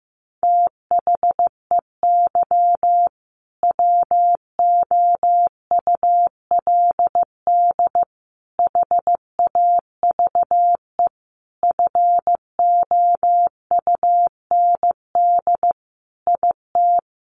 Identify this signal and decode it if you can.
Morse code, heard as a keyed tone, about 15 wpm, 700 Hz. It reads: THEY WOULD HAVE FOUND IT
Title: Sentences from Top 1400 Words 15wpm